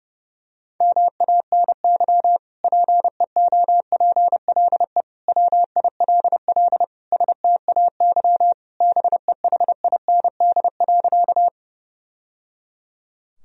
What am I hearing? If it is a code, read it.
Morse code, heard as a keyed tone, about 32 wpm, 700 Hz. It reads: MANY PEOPLE WILL STAY BEHIND.
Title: Sentences from Top 1000 Words - Review Part 1 - 30wpm